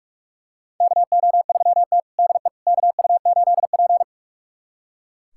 Morse code, heard as a keyed tone, about 43 wpm, 700 Hz. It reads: KO3T DE KU8P
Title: USA Callsign DE Callsign - 4 Characters - 45wpm